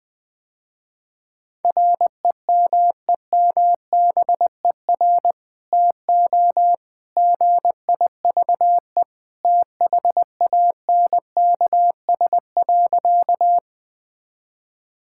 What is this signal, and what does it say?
Morse code, heard as a keyed tone, about 20 wpm, 700 Hz. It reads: REMEMBER TO GIVE THANKS.
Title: Sentences with 2-Word Phrases - Spoken Beginning - 20wpm